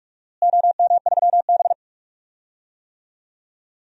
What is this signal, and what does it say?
Morse code, heard as a keyed tone, about 43 wpm, 700 Hz. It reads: OM3B
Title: DXCC-100 Call Signs - 4 Characters - Repeated 3x - 45wpm